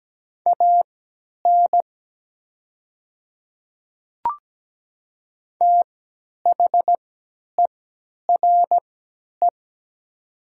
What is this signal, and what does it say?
Morse code, heard as a keyed tone, about 17 wpm, spaced as for 10 wpm, 700 Hz. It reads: AN THERE
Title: 3x Character Spacing - Top 100 Words - 17wpm